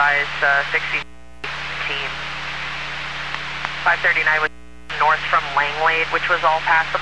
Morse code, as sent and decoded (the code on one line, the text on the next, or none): none